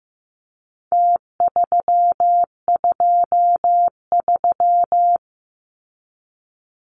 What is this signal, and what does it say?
Morse code, heard as a keyed tone, about 15 wpm, 700 Hz. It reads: T323
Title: Numbers - Contest Style - 4-Digit 15wpm